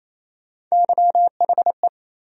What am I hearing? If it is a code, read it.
Morse code, heard as a keyed tone, about 28 wpm, 700 Hz. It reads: YHE
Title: USA Call Sign Postfixes - 3 Letters 28wpm